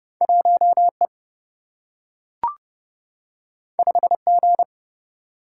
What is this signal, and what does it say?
Morse code, heard as a keyed tone, about 30 wpm, 700 Hz. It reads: 1E 5G
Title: Single-Letter Single-Number 30wpm